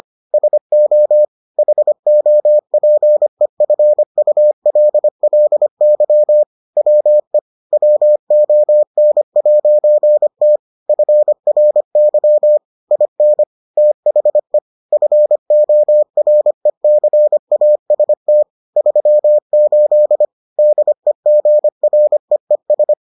Morse code, heard as a keyed tone, about 25 wpm, 600 Hz. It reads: SO HOPEFULLY WE WON'T FRY IN THE FORECAST 38 DEGREES